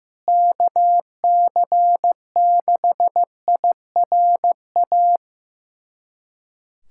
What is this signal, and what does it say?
Morse code, heard as a keyed tone, about 16 wpm, 700 Hz. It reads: KC6IRA
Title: USA Callsign DE Callsign 15wpm